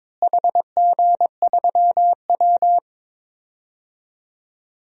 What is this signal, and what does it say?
Morse code, heard as a keyed tone, about 22 wpm, 700 Hz. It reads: HG3W
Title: DXCC-100 Call Signs - 4 Characters - Repeated 3x - 22wpm